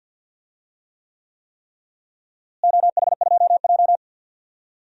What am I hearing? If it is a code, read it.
Morse code, heard as a keyed tone, about 52 wpm, 700 Hz. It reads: OH2J
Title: DXCC-100 Call Signs - 4 Characters - Repeated 3x - 50wpm